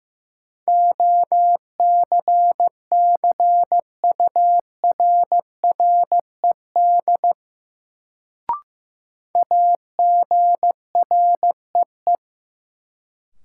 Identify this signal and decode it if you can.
Morse code, heard as a keyed tone, about 15 wpm, 700 Hz. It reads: OCCURRED AGREE
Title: Sets of 5 Words Encore - New Word Drill - 15wpm